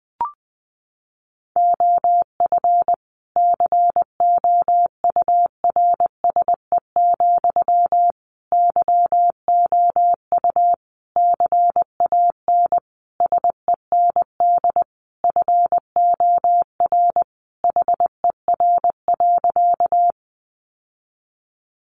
Morse code, heard as a keyed tone, about 20 wpm, 700 Hz. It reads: OF COURSE, YOU CAN SEND FOR HER.
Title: Sentences from Top 1500 Words 20wpm